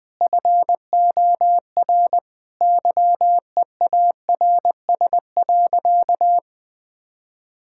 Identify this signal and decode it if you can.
Morse code, heard as a keyed tone, about 21 wpm, 700 Hz. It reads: FOR YEARS.
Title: Sentences with 6-Word Phrases - Spoken Beginning - 20wpm